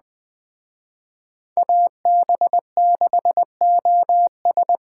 Morse code, heard as a keyed tone, about 20 wpm, 700 Hz. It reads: AB6OS